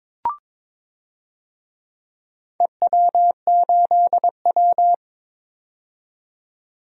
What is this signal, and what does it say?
Morse code, heard as a keyed tone, about 22 wpm, 700 Hz. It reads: EW8W
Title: DXCC-100 Call Signs - 4 Characters - 22wpm